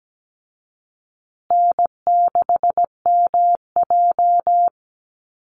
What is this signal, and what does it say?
Morse code, heard as a keyed tone, about 17 wpm, 700 Hz. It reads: N6MJ
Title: Easy Call Signs Repeated 3x - 17wpm